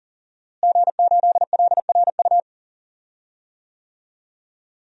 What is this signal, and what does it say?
Morse code, heard as a keyed tone, about 40 wpm, 700 Hz. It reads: G8LRU
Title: DXCC-100 Call Signs - 5 Characters - Repeated 3x - 40wpm